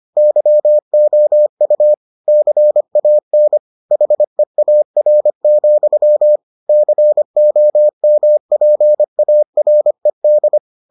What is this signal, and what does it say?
Morse code, heard as a keyed tone, about 25 wpm, 600 Hz. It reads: YOU CAN HEAR, COMPARED